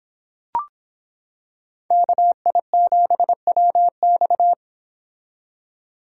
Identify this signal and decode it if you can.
Morse code, heard as a keyed tone, about 26 wpm, 700 Hz. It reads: KI7WX